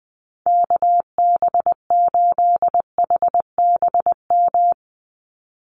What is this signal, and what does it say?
Morse code, heard as a keyed tone, about 20 wpm, 700 Hz. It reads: KB8HBM
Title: USA Call Signs 20wpm